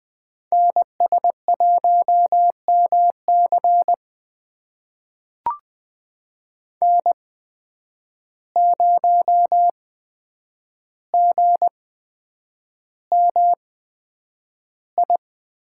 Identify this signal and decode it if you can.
Morse code, heard as a keyed tone, about 20 wpm, spaced as for 9 wpm, 700 Hz. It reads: NS1MC N0GMI
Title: USA Call Signs - 5 Characters - 8x Character Spacing - 20wpm